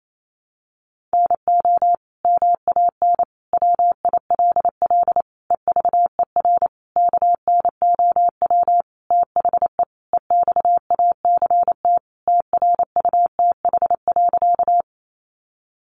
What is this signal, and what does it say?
Morse code, heard as a keyed tone, about 28 wpm, 700 Hz. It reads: NO MAN WILL EVER KNOW THE EXACT TRUTH.
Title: Sentences from Top 1300 Words 28wpm